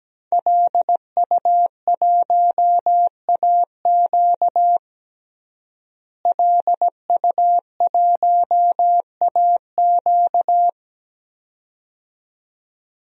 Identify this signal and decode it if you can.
Morse code, heard as a keyed tone, about 17 wpm, 700 Hz. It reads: LU1AQ LU1AQ
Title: DXCC-100 Call Signs - 5 Characters - Repeated 2x - 17wpm